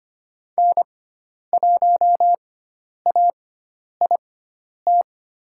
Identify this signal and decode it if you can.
Morse code, heard as a keyed tone, about 25 wpm, spaced as for 10 wpm, 700 Hz. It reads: N1AIT
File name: Lesson 16g - Letter c - Call Signs 25wpm